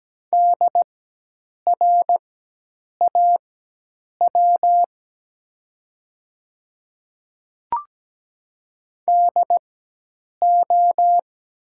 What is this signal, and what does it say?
Morse code, heard as a keyed tone, about 17 wpm, spaced as for 8 wpm, 700 Hz. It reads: DRAW DO
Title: 4x Character Spacing - Top 300 Words - 17wpm